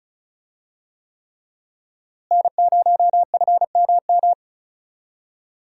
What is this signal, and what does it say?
Morse code, heard as a keyed tone, about 36 wpm, 700 Hz. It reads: N0FMM
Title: USA Call Signs - 5 Characters - Repeated 3x - 35wpm